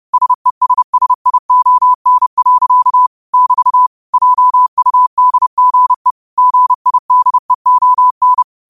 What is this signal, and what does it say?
Morse code, heard as a keyed tone, about 30 wpm, 1000 Hz. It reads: SESSION. <BT> JUDGE GIDEON
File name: qotdfiles / 30wpm